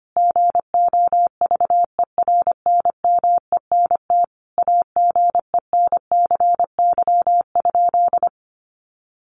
Morse code, heard as a keyed tone, about 25 wpm, 700 Hz. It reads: GOVERNMENT AGENCY?
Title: Sentences from Top 1300 Words 25wpm